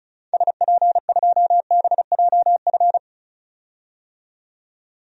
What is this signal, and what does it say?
Morse code, heard as a keyed tone, about 34 wpm, 700 Hz. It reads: SP2BJF
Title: DXCC-100 Call Signs - 6 Characters - 35wpm